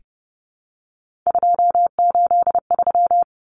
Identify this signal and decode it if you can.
Morse code, heard as a keyed tone, about 30 wpm, 700 Hz. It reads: 283
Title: Common Cut Numbers - 3-Digit 30wpm